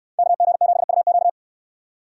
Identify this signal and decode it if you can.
Morse code, heard as a keyed tone, about 70 wpm, 700 Hz. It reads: KG7UQ